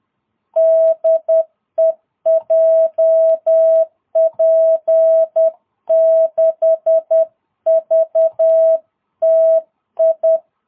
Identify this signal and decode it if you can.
Morse code, heard as a keyed tone, about 10 wpm, 660 Hz. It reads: DE JP6VTI
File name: jh8cbh / 2022cw